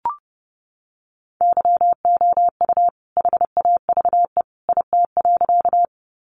Morse code, heard as a keyed tone, about 28 wpm, 700 Hz. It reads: YOU HAVE IT.
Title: Sentences from Top 100 Words 30wpm